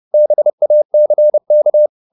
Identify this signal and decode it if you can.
Morse code, heard as a keyed tone, about 30 wpm, 600 Hz. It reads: BACK
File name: the old man and the sea / the old man and the sea 1 30wpm